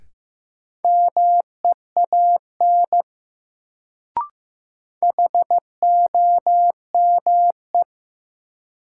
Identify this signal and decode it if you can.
Morse code, heard as a keyed tone, about 15 wpm, 700 Hz. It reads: MEAN HOME
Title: Sentences from Top 200 Words - New Word Drill - 15wpm